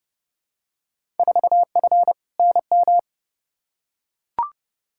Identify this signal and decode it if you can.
Morse code, heard as a keyed tone, about 32 wpm, 700 Hz. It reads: 4F NM
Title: Field Day Exchanges Encore - 30wpm